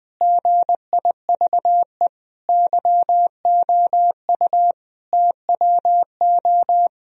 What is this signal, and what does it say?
Morse code, heard as a keyed tone, about 20 wpm, 700 Hz. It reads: GIVE YOU TWO
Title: Sentences from Top 1000 Words - Review Part 2 - 20wpm